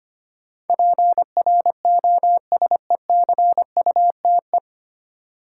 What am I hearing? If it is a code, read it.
Morse code, heard as a keyed tone, about 25 wpm, 700 Hz. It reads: PROSECUTE
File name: practice / 25wpm / 9 Letter Words - First 3 Letters Given - 25wpm